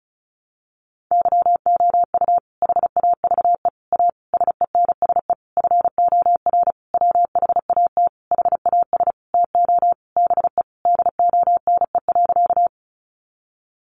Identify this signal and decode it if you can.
Morse code, heard as a keyed tone, about 34 wpm, 700 Hz. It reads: YOU HAVE A SENSE FOR WHAT HAS TO BE DONE.
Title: Sentences from Top 700 Words 35wpm